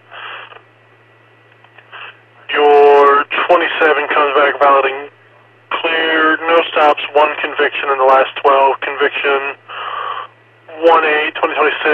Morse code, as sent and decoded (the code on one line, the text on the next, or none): none